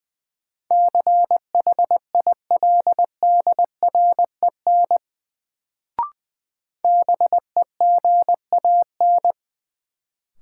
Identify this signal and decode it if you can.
Morse code, heard as a keyed tone, about 20 wpm, 700 Hz. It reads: CHILDREN BEGAN
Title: Sentences from Top 300 Words - New Word Drill - 20wpm